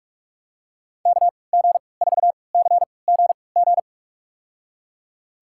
Morse code, heard as a keyed tone, about 47 wpm, spaced as for 26 wpm, 700 Hz. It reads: KG4CGG
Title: USA Call Signs - 6 Characters - 3x Character Spacing - 45wpm